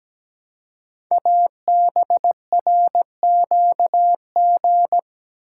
Morse code, heard as a keyed tone, about 17 wpm, 700 Hz. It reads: ABRQG
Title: Code Groups - 5 Letters - 17wpm